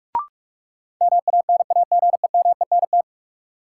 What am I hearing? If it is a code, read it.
Morse code, heard as a keyed tone, about 46 wpm, 700 Hz. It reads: MANAGEMENT
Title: Sentences from Top 1200 Words - New Word Drill Alt - 45wpm